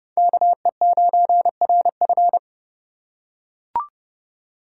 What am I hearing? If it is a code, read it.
Morse code, heard as a keyed tone, about 30 wpm, 700 Hz. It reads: KE9RF
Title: USA Call Signs - 3x Character Spacing - 30wpm